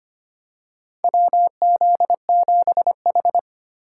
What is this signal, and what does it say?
Morse code, heard as a keyed tone, about 25 wpm, 700 Hz. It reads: WZ7H
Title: USA Call Signs - 4x Character Spacing - 25wpm